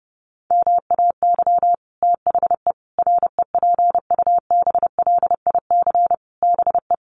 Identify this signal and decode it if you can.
Morse code, heard as a keyed tone, about 30 wpm, 700 Hz. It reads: MAY THE REPUBLIC BE
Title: Sentences from Top 1500 Words 30wpm